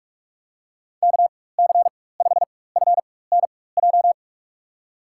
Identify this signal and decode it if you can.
Morse code, heard as a keyed tone, about 46 wpm, spaced as for 21 wpm, 700 Hz. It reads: KC5FNJ
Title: USA Call Signs - 6 Characters - 4x Character Spacing - 45wpm